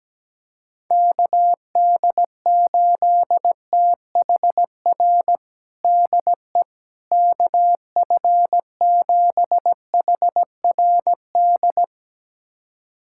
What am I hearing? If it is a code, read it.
Morse code, heard as a keyed tone, about 17 wpm, 700 Hz. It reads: KD8THR DE KF7HRD
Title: USA Callsign DE Callsign - 6 Characters - 17wpm